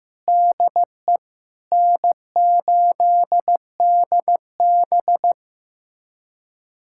Morse code, heard as a keyed tone, about 15 wpm, 700 Hz. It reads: DE N8DB
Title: USA Callsign DE Callsign - 4 Characters - 15wpm